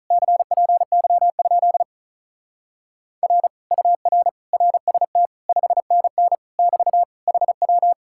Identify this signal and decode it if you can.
Morse code, heard as a keyed tone, about 43 wpm, 700 Hz. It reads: CPY? R UR RST 5NN <BT> HW
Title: Speed-Racing Common QSO Phrases - 35wpm